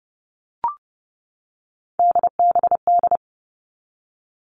Morse code, heard as a keyed tone, about 30 wpm, 700 Hz. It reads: DBD
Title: All Cut Numbers - 3-Digit 30wpm